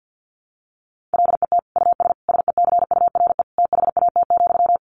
Morse code, heard as a keyed tone, about 50 wpm, 700 Hz. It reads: LET US SECURE THAT,
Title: Sentences from Top 1200 Words 50wpm